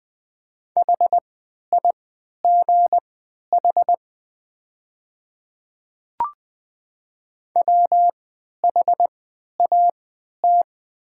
Morse code, heard as a keyed tone, about 20 wpm, spaced as for 11 wpm, 700 Hz. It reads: HIGH WHAT
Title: 3x Character Spacing - Top 200 Words - 20wpm